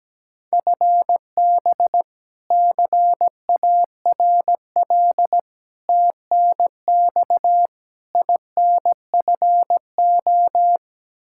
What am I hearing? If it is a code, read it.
Morse code, heard as a keyed tone, about 17 wpm, 700 Hz. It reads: FB CARL TNX INFO
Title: Practice QSO - Exchange 3 - More Info - 17wpm